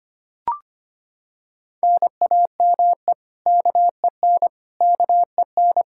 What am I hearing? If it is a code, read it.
Morse code, heard as a keyed tone, about 25 wpm, 700 Hz. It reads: NAME KEN KEN
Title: Practice QSO - Exchange 1 - Beginning - 25wpm